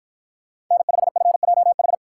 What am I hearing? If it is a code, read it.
Morse code, heard as a keyed tone, about 48 wpm, 700 Hz. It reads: N5FJH